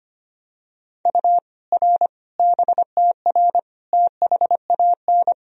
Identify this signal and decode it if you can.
Morse code, heard as a keyed tone, about 25 wpm, 700 Hz. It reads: U R BTR THAN
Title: Sentences from Top 400 Words and Abbreviations 25wpm